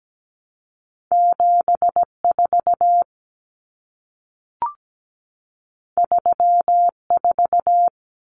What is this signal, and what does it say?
Morse code, heard as a keyed tone, about 17 wpm, 700 Hz. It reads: 74 34
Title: Common Cut Numbers - 2-Digit 17wpm